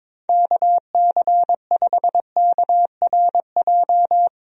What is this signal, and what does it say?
Morse code, heard as a keyed tone, about 22 wpm, 700 Hz. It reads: KC5KRJ
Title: USA Call Signs - 6 Characters - 22wpm